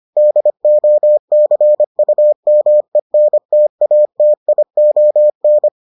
Morse code, heard as a keyed tone, about 25 wpm, 600 Hz. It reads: DOCUMENTATION